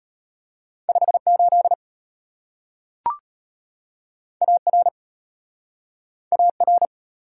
Morse code, heard as a keyed tone, about 34 wpm, 700 Hz. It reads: 58 AR AR